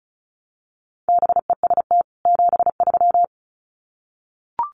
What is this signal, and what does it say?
Morse code, heard as a keyed tone, about 34 wpm, 700 Hz. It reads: BEST 73
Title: Common QSO Phrases 35wpm